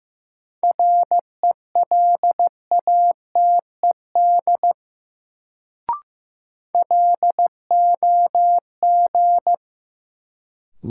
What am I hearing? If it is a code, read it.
Morse code, heard as a keyed tone, about 15 wpm, 700 Hz. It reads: RELATED LOG
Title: Sentences from Top 1400 Words - New Word Drill Alt - 15wpm